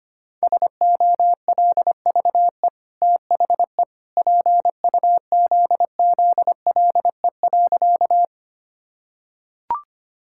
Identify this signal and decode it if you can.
Morse code, heard as a keyed tone, about 25 wpm, 690 Hz. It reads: SOLVE THE PUZZLE.
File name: Easy Sentences with Hints - 25wpm